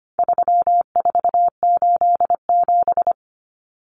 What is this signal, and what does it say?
Morse code, heard as a keyed tone, about 25 wpm, 700 Hz. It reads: 3487
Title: Numbers - 4-Digit 25wpm